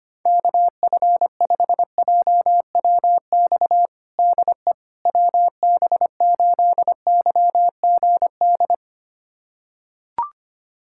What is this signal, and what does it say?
Morse code, heard as a keyed tone, about 25 wpm, 700 Hz. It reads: KF5JWX DE WB8YGD
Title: USA Callsign DE Callsign 25wpm